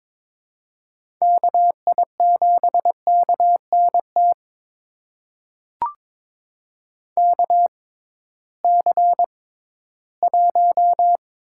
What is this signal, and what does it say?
Morse code, heard as a keyed tone, about 22 wpm, spaced as for 14 wpm, 700 Hz. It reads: KI7KNT KC1
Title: USA Call Signs - 6 Characters - 6x Character Spacing - 22wpm